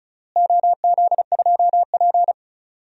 Morse code, heard as a keyed tone, about 34 wpm, 700 Hz. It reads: OZ2P
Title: DXCC-100 Call Signs - 4 Characters - Repeated 3x - 35wpm